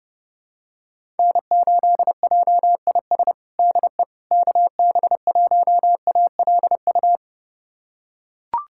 Morse code, heard as a keyed tone, about 30 wpm, 700 Hz. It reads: N8JIS DE KB1ALU
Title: USA Callsign DE Callsign 30wpm